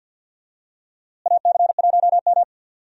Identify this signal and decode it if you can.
Morse code, heard as a keyed tone, about 52 wpm, 700 Hz. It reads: AC1M
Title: USA Call Signs - 4 Characters - Repeated 3x - 50wpm